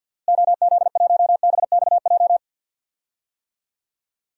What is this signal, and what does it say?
Morse code, heard as a keyed tone, about 52 wpm, 700 Hz. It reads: OZ1BXJ
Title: DXCC-100 Call Signs - 6 Characters - Repeated 3x - 50wpm